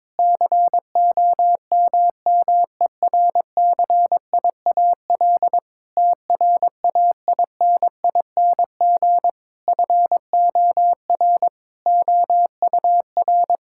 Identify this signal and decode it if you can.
Morse code, heard as a keyed tone, about 21 wpm, 700 Hz. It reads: COMMERCIAL TRAINING FOR OUR
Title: Sentences from Top 1000 Words - Review Part 5 - 22wpm